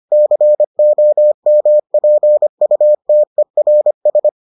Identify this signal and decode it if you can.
Morse code, heard as a keyed tone, about 25 wpm, 600 Hz. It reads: COMPUTERS